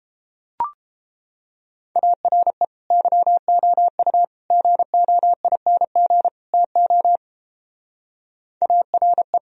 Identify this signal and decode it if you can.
Morse code, heard as a keyed tone, about 29 wpm, 700 Hz. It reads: ARE YOU GOING TO ARE